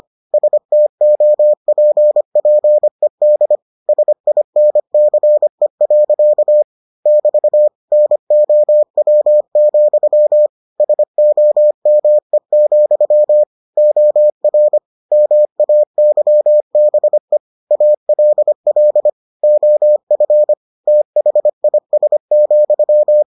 Morse code, heard as a keyed tone, about 25 wpm, 600 Hz. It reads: STOPPED SINCE. <BT> NOW, SOME, OR MAYBE ALL OF THIS,